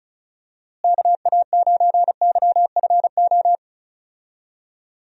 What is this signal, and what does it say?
Morse code, heard as a keyed tone, about 36 wpm, 700 Hz. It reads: KA9YFO
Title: USA Call Signs - 6 Characters - Repeated 2x - 35wpm